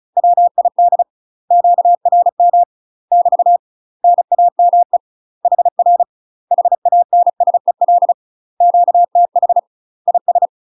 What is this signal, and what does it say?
Morse code, heard as a keyed tone, about 35 wpm, 700 Hz. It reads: WID QRM <BT> NAME HR HANSEL QTH IS
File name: UK MASTER QSOs TRAINING / 35WPM / UK MASTER QSOs 0032